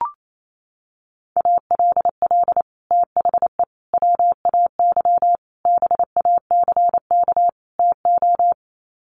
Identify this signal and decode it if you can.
Morse code, heard as a keyed tone, about 28 wpm, 700 Hz. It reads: ALL THE WAY BACK TO
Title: Sets of 5 Words Encore 28wpm